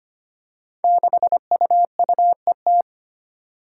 Morse code, heard as a keyed tone, about 25 wpm, 700 Hz. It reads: 6UUET
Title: Five Challenging Letter-Number Combinations 25wpm